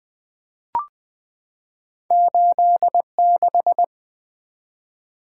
Morse code, heard as a keyed tone, about 20 wpm, 700 Hz. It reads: 86
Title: Two Letter-Number Combinations - 20wpm